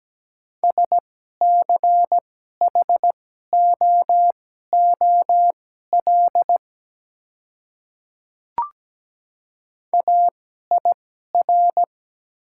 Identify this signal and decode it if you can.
Morse code, heard as a keyed tone, about 17 wpm, spaced as for 12 wpm, 700 Hz. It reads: SCHOOL AIR
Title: 2x Character Spacing - Top 300 Words - 17wpm